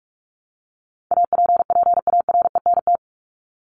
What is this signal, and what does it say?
Morse code, heard as a keyed tone, about 44 wpm, 700 Hz. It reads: APPARENT
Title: LY Suffix - Top 100 Words with Base Words 45wpm